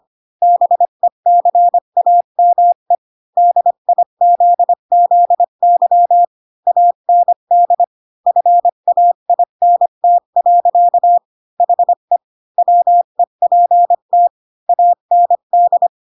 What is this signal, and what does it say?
Morse code, heard as a keyed tone, about 25 wpm, 700 Hz. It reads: BECAME DIZZY AND FAINT. HE WEPT AND
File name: Pinocchio / Pinocchio ch5 20 25 ik1hge